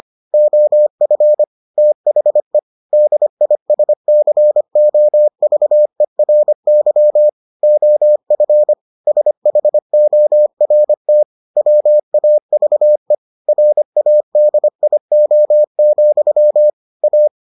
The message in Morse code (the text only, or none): OF THE DISCOVERY OF SHORT WAVE RADIO, A